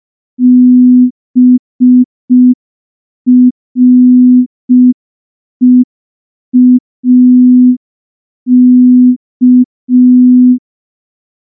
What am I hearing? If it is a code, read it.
Morse code, heard as a keyed tone, about 5 wpm, 250 Hz. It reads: BREAK
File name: morse / morse-training-05WPM